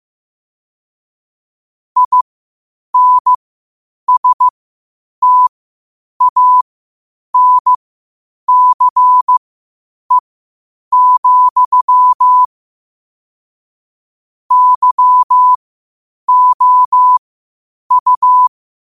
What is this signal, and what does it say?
Morse code, heard as a keyed tone, about 15 wpm, spaced as for 8 wpm, 1000 Hz. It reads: INSTANCE, YOU